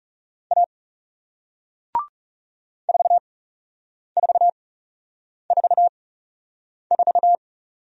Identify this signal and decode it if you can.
Morse code, heard as a keyed tone, about 38 wpm, spaced as for 8 wpm, 700 Hz. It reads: A 4444